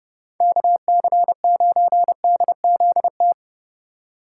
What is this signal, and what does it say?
Morse code, heard as a keyed tone, about 37 wpm, 700 Hz. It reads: KC9DZT
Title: Speed Racing - USA 2x3 Call Signs 30wpm